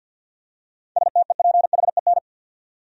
Morse code, heard as a keyed tone, about 51 wpm, 700 Hz. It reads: STEPHEN
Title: USA Common Names - 50wpm